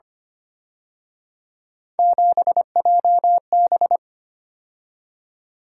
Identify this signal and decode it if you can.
Morse code, heard as a keyed tone, about 29 wpm, 700 Hz. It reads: 7JB
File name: Speed-Racing - Three Letter-Number Combinations - 25wpm